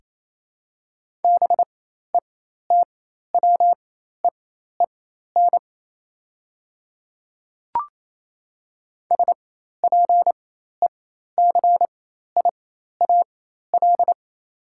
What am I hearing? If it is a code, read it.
Morse code, heard as a keyed tone, about 27 wpm, spaced as for 13 wpm, 700 Hz. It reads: BETWEEN SPECIAL
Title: 4x Character Spacing - Top 500 Words - 28wpm